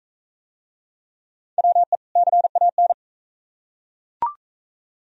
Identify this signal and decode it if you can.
Morse code, heard as a keyed tone, about 38 wpm, 700 Hz. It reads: WE CAN